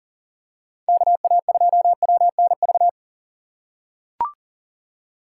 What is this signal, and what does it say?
Morse code, heard as a keyed tone, about 40 wpm, 700 Hz. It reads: KA2WNV
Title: USA Call Signs - 6 Characters - 3x Character Spacing - 40wpm